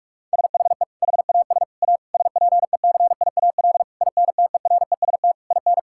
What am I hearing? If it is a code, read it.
Morse code, heard as a keyed tone, about 41 wpm, 700 Hz. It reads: SHE HAS A SPECIAL INTEREST IN
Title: Sentences from Top 1000 Words - Review Part 5 - 45wpm